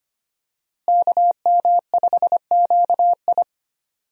Morse code, heard as a keyed tone, about 25 wpm, 700 Hz. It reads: KM5QI
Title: USA Call Signs - 5 Characters - Repeated 3x - 25wpm